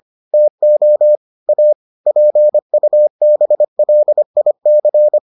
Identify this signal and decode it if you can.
Morse code, heard as a keyed tone, about 25 wpm, 600 Hz. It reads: TO A PUBLIC